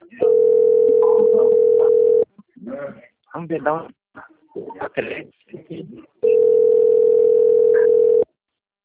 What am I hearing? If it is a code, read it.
Ringback tone (North America), 440 + 480 Hz, with speech.
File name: out- - - - - .22724